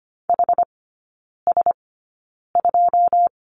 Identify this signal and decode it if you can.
Morse code, heard as a keyed tone, about 25 wpm, spaced as for 9 wpm, 690 Hz. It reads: HS2
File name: Lesson 14e - Number 2 - Three Characters 25wpm